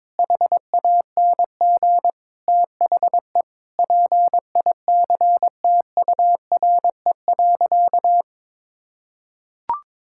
Morse code, heard as a keyed tone, about 21 wpm, 700 Hz. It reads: HANG THE PICTURE.
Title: Sentences from Top 1600 Words 22wpm